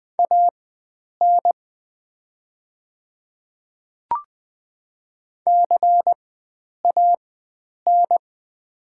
Morse code, heard as a keyed tone, about 20 wpm, spaced as for 9 wpm, 700 Hz. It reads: AN CAN
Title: 4x Character Spacing - Top 100 Words - 20wpm